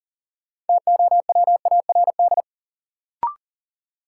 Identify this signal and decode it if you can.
Morse code, heard as a keyed tone, about 40 wpm, 700 Hz. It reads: TOWARD
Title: Sentences from Top 1000 Words - Review Part 6 - New Word Drill - 40wpm